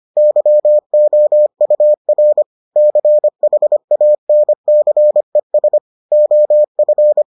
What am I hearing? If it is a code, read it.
Morse code, heard as a keyed tone, about 25 wpm, 600 Hz. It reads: YOUR CHANCES OF